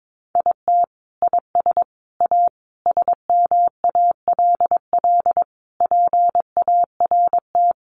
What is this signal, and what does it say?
Morse code, heard as a keyed tone, about 22 wpm, 700 Hz. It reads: IT IS A SMALL PART
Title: Sentences from Top 1500 Words 22wpm